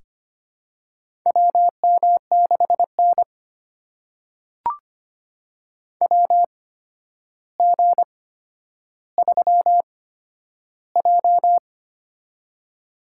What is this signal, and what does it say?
Morse code, heard as a keyed tone, about 25 wpm, spaced as for 11 wpm, 700 Hz. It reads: WM6N WG3J
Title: USA Call Signs - 8x Character Spacing - 25wpm